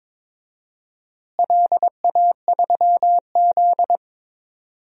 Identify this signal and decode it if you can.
Morse code, heard as a keyed tone, about 22 wpm, 700 Hz. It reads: LA3Z
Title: DXCC-100 Call Signs - 4 Characters - Repeated 3x - 22wpm